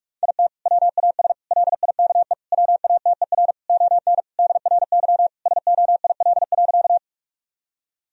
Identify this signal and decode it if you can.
Morse code, heard as a keyed tone, about 41 wpm, 700 Hz. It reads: IT WAS LIKE WATER ON DRY SOIL.
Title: Sentences from Top 1000 Words - Review Part 4 - 45wpm